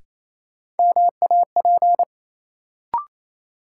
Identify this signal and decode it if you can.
Morse code, heard as a keyed tone, about 28 wpm, 700 Hz. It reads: MAP
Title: Sentences from Top 400 Words - New Word Drill - 28wpm